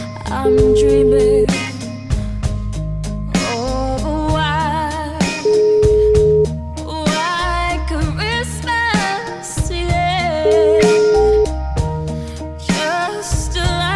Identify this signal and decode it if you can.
Ringback tone (Europe, 425 Hz), with noise.